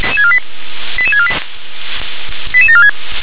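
Five-tone selective call (ZVEI), heard as quick runs of five tones, tones of about 70 ms. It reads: 88648 88746
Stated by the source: ZVEI five-tone selective call received by SDR.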